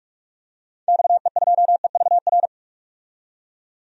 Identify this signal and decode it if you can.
Morse code, heard as a keyed tone, about 46 wpm, 700 Hz. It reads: XE2EVR
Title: DXCC-100 Call Signs - 6 Characters - 45wpm